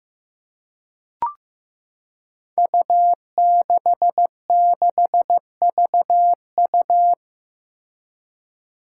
Morse code, heard as a keyed tone, about 15 wpm, 700 Hz. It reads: U66VU
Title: Five Challenging Letter-Number Combinations 15wpm